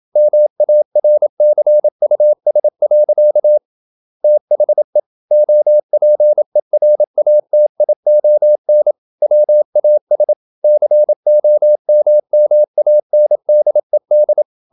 Morse code, heard as a keyed tone, about 27 wpm, 600 Hz. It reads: MARCUS. THE OPERATION WAS COMMANDED
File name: Never 0027wpm